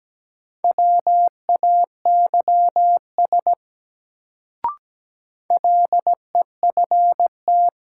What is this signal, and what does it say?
Morse code, heard as a keyed tone, about 17 wpm, 700 Hz. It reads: WAYS LEFT
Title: Sets of 5 Words Encore - New Word Drill - 17wpm